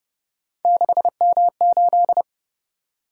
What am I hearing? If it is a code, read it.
Morse code, heard as a keyed tone, about 31 wpm, 700 Hz. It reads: 6M8
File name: Speed-Racing - Three Letter-Number Combinations - 25wpm